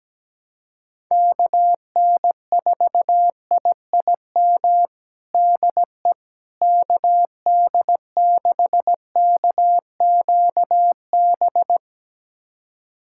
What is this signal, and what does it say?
Morse code, heard as a keyed tone, about 17 wpm, 700 Hz. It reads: KN4IIM DE KD6KQB
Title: USA Callsign DE Callsign 17wpm